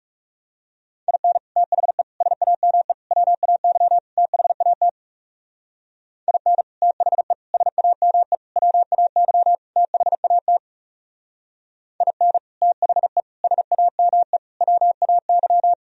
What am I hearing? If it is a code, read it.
Morse code, heard as a keyed tone, about 34 wpm, 700 Hz. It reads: IN THE SAME WAY THAT IN THE SAME WAY THAT IN THE SAME WAY
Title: Speed-Racing - Sets of 5 Words 30wpm